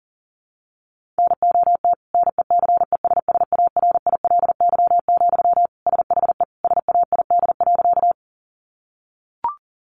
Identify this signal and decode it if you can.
Morse code, heard as a keyed tone, about 37 wpm, 700 Hz. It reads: NOT NECESSARILY, SHE SAID.
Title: Sentences from Top 1400 Words 40wpm